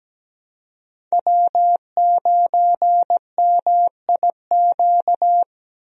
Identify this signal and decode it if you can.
Morse code, heard as a keyed tone, about 17 wpm, 700 Hz. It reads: W9MIQ
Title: USA Call Signs - 5 Characters - 17wpm